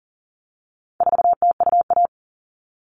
Morse code, heard as a keyed tone, about 40 wpm, 700 Hz. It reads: 4TUA
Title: All Cut Numbers - 4-Digit 40wpm